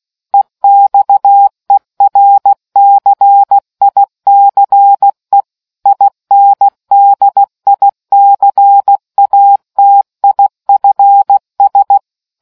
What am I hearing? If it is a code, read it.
Morse code, heard as a keyed tone, about 16 wpm, 800 Hz. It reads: EXERCICE INDICATIFS